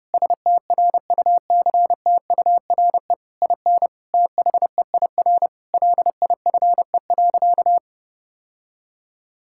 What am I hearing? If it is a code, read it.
Morse code, heard as a keyed tone, about 32 wpm, 700 Hz. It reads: STRUCTURE IN THEIR LIFE.
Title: Sentences from Top 1000 Words - Review Part 6 - 30wpm